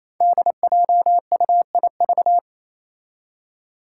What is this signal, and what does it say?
Morse code, heard as a keyed tone, about 27 wpm, 700 Hz. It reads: DJUIV
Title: Five Challenging Letter-Number Combinations 28wpm